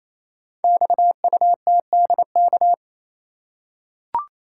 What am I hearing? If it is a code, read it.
Morse code, heard as a keyed tone, about 29 wpm, 700 Hz. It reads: XUTDK
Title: Code Groups - 5 Letters - 28wpm